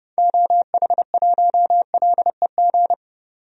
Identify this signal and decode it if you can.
Morse code, heard as a keyed tone, about 30 wpm, 700 Hz. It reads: OH1LEG
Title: DXCC-100 Call Signs - 6 Characters - 30wpm